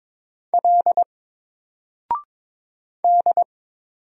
Morse code, heard as a keyed tone, about 22 wpm, 700 Hz. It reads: L D
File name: Single-Letter-Number-22wpm